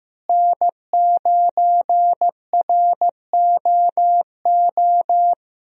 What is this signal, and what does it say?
Morse code, heard as a keyed tone, about 16 wpm, 700 Hz. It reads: N9ROO
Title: USA Call Signs - 5 Characters - Repeated 3x - 15wpm